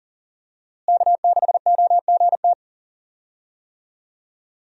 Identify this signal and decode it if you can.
Morse code, heard as a keyed tone, about 40 wpm, 700 Hz. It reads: K6OGT
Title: USA Call Signs - 5 Characters - Repeated 3x - 40wpm